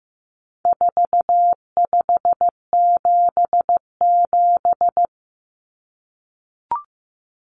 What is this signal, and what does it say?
Morse code, heard as a keyed tone, about 15 wpm, 700 Hz. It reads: 4577
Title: Common Cut Numbers - 4-Digit 15wpm